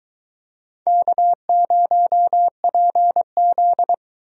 Morse code, heard as a keyed tone, about 22 wpm, 700 Hz. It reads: K0PZ